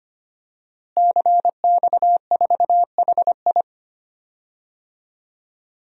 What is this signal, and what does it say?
Morse code, heard as a keyed tone, about 25 wpm, 700 Hz. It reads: CX4HI
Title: DXCC-100 Call Signs - 5 Characters - Repeated 2x - 25wpm